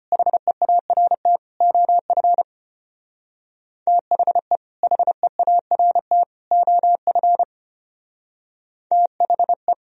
Morse code, heard as a keyed tone, about 29 wpm, 700 Hz. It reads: HEART OF THE HEART OF THE
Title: Speed-Racing - Sets of 3 Words 25wpm